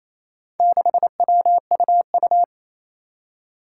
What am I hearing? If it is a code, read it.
Morse code, heard as a keyed tone, about 28 wpm, 700 Hz. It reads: 6WUU
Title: Four Challenging Letter-Number Combinations 28wpm